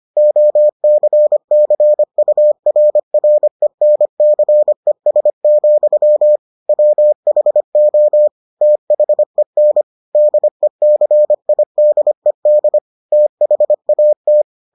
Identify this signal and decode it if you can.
Morse code, heard as a keyed tone, about 25 wpm, 600 Hz. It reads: OCCURRENCES, WHO THEN DECIDED THAT